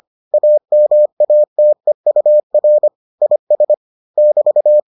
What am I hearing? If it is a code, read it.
Morse code, heard as a keyed tone, about 25 wpm, 600 Hz. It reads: AMATEUR IS <BT>